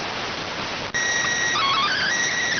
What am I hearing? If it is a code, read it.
SSTV (slow-scan television), PD 290.